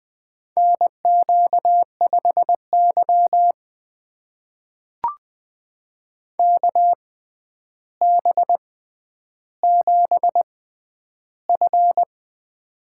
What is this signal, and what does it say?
Morse code, heard as a keyed tone, about 20 wpm, spaced as for 10 wpm, 700 Hz. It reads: NQ5Y KB7F
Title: USA Call Signs - 4 Characters - 6x Character Spacing - 20wpm